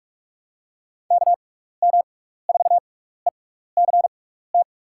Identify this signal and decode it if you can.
Morse code, heard as a keyed tone, about 42 wpm, spaced as for 15 wpm, 700 Hz. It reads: KM4ECT
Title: USA Call Signs - 6 Characters - 6x Character Spacing - 45wpm